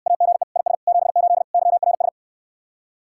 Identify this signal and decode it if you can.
Morse code, heard as a keyed tone, about 69 wpm, 700 Hz. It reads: AGE IS 78 YRS